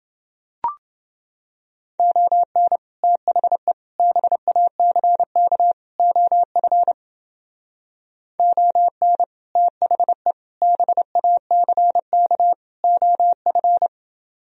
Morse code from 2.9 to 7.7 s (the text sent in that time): THE BACK OF